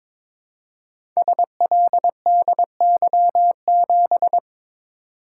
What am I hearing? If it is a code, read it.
Morse code, heard as a keyed tone, about 22 wpm, 700 Hz. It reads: SLDY7
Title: Code Groups - 5 Letter:Numbers - 22wpm